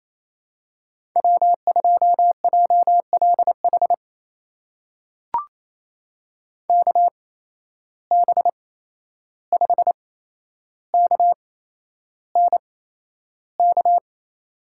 Morse code, heard as a keyed tone, about 28 wpm, spaced as for 11 wpm, 700 Hz. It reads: W2JLH KB5KNK
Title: USA Call Signs - 8x Character Spacing - 28wpm